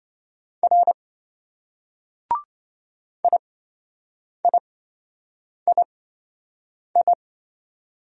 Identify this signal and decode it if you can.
Morse code, heard as a keyed tone, about 27 wpm, spaced as for 7 wpm, 700 Hz. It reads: R IIII